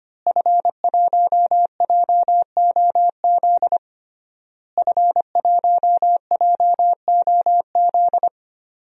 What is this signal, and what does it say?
Morse code, heard as a keyed tone, about 25 wpm, 700 Hz. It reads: F1JOZ F1JOZ
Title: DXCC-100 Call Signs - 5 Characters - Repeated 3x - 25wpm